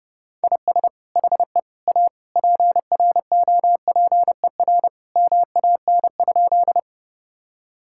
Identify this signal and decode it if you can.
Morse code, heard as a keyed tone, about 32 wpm, 700 Hz. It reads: IS HE A PROPER MAN?
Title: Sentences from Top 1000 Words - Review Part 5 - 30wpm